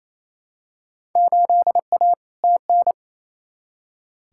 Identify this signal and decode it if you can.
Morse code, heard as a keyed tone, about 28 wpm, 700 Hz. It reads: 8A TN
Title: Field Day Exchanges Encore - 28wpm